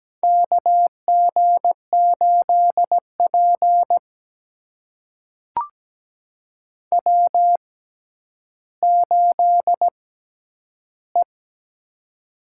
Morse code, heard as a keyed tone, about 17 wpm, spaced as for 10 wpm, 700 Hz. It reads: KG8P W8E
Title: USA Call Signs - 4 Characters - 6x Character Spacing - 17wpm